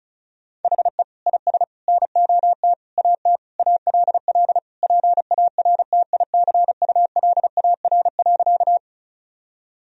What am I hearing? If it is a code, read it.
Morse code, heard as a keyed tone, about 34 wpm, 700 Hz. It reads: HE IS NOT AT ALL PARTICULAR.
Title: Sentences from Top 1000 Words - Review Part 4 - 35wpm